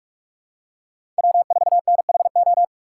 Morse code, heard as a keyed tone, about 46 wpm, 700 Hz. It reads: W4NHO
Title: USA Call Signs - 6x Character Spacing - 45wpm